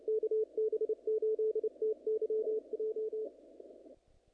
Morse code, heard as a keyed tone, about 29 wpm, 440 Hz. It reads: KB8TYJ
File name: contests / naqp-cw-k5tr-op-k5pi- -1900